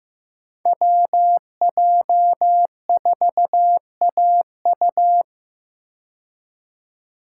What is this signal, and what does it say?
Morse code, heard as a keyed tone, about 15 wpm, 700 Hz. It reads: WJ4AU
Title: Code Groups - 5 Letter:Numbers - 15wpm